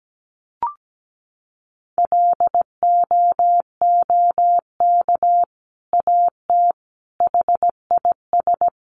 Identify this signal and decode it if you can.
Morse code, heard as a keyed tone, about 17 wpm, 700 Hz. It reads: LOOK AT HIS
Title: Sentences from Top 1500 Words 17wpm